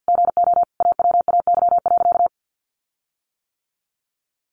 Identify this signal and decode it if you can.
Morse code, heard as a keyed tone, about 48 wpm, 700 Hz. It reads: GO AWAY.
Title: Sentences from Top 1500 Words 50wpm